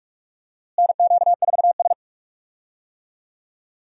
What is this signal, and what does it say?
Morse code, heard as a keyed tone, about 46 wpm, 700 Hz. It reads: NQ4S
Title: USA Call Signs - 4 Characters - Repeated 2x - 45wpm